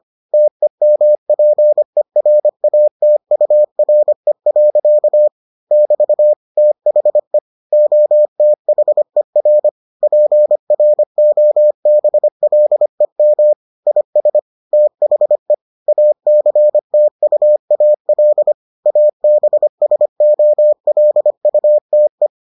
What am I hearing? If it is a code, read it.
Morse code, heard as a keyed tone, about 25 wpm, 600 Hz. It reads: TEMPERATURE. <BT> THE OTHER PROBLEM IS THE ACTUAL ABSOLUTE